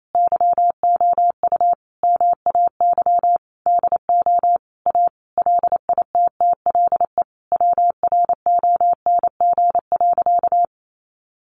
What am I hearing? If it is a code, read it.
Morse code, heard as a keyed tone, about 28 wpm, 690 Hz. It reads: YOU MAY DO A LITTLE WRONG.
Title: Sentences from Top 900 Words 28wpm